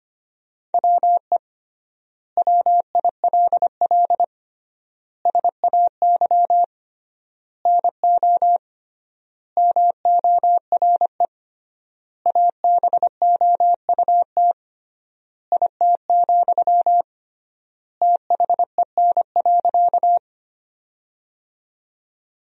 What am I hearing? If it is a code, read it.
Morse code, heard as a keyed tone, about 25 wpm, 700 Hz. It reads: WE WILL SAY NO MORE ABOUT IT, THEN.
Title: 3x Word-Spacing - Sentences from Top 200 Words - 25wpm